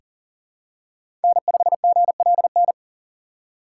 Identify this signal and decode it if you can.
Morse code, heard as a keyed tone, about 40 wpm, 700 Hz. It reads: N5GLN
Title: USA Call Signs - 5 Characters - 6x Character Spacing - 40wpm